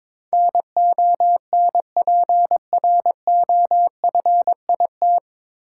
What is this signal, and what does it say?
Morse code, heard as a keyed tone, about 22 wpm, 700 Hz. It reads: NONPROFIT
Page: Morse Code Ninja practice 22wpm